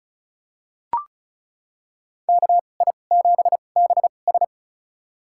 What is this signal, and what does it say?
Morse code, heard as a keyed tone, about 34 wpm, spaced as for 25 wpm, 700 Hz. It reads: KI7BS
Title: USA Call Signs - 5 Characters - 2x Character Spacing - 35wpm